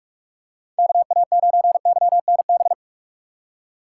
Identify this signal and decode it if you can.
Morse code, heard as a keyed tone, about 47 wpm, 700 Hz. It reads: KA9YNB